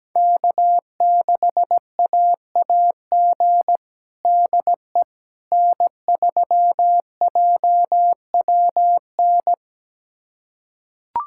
Morse code, heard as a keyed tone, about 17 wpm, 700 Hz. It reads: K6AAG DE N3JWN
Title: USA Callsign DE Callsign - 5 Characters - 17wpm